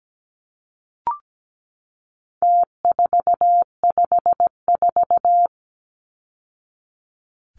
Morse code, heard as a keyed tone, about 17 wpm, 700 Hz. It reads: T454
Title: Numbers - Contest Style - 4-Digit 17wpm